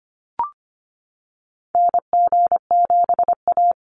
Morse code, heard as a keyed tone, about 25 wpm, 700 Hz. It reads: NG7A
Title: NA SOTA Activator Callsigns 25wpm